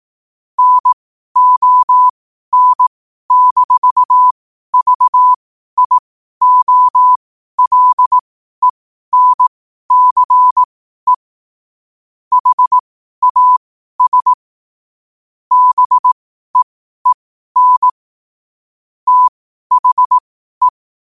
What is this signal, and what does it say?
Morse code, heard as a keyed tone, about 18 wpm, spaced as for 12 wpm, 1000 Hz. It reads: NON-VIOLENCE HAS BEEN THE
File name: qotdfiles / 12wpm